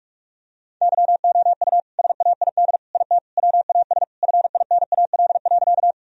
Morse code, heard as a keyed tone, about 46 wpm, 700 Hz. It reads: YOU SAID IT WAS FINAL.
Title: Sentences from Top 1000 Words - Review Part 2 - 45wpm